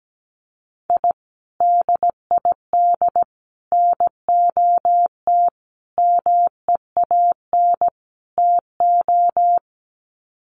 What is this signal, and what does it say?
Morse code, heard as a keyed tone, about 17 wpm, 700 Hz. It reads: I DID NOT MEAN TO
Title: Sets of 5 Words 17wpm